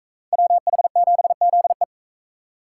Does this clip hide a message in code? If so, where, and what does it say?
Yes: Morse code from 0.3 to 1.8 s, heard as a keyed tone, about 37 wpm, 700 Hz. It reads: WH7ZE